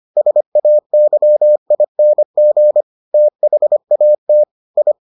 Morse code, heard as a keyed tone, about 25 wpm, 600 Hz. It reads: SAYING THAT I